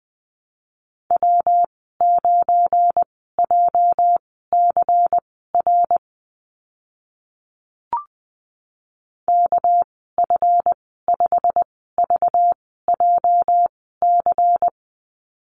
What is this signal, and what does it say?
Morse code, heard as a keyed tone, about 20 wpm, spaced as for 14 wpm, 700 Hz. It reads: W9JCR KF5VJC
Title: 245 Lesson 39g - Letter z - Call Signs 20wpm